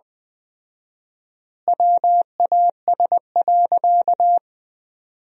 Morse code, heard as a keyed tone, about 20 wpm, 700 Hz. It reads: WAS.
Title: 4x Word-Spacing - Sentences from Top 300 Words - 20wpm